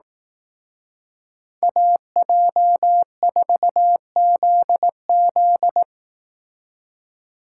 Morse code, heard as a keyed tone, about 19 wpm, 700 Hz. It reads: AJ4ZZ
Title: Speed Racing - USA 2x2 Call Signs 15wpm